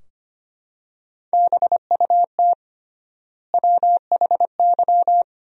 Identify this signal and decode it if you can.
Morse code, heard as a keyed tone, about 25 wpm, 700 Hz. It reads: BUT WHY
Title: 3x Word-Spacing - Sentences from Top 400 Words - 25wpm